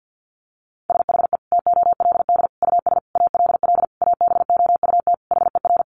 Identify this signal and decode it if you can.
Morse code, heard as a keyed tone, about 50 wpm, 700 Hz. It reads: SHE TOLD US ALL ABOUT HER
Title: Sentences from Top 1200 Words 50wpm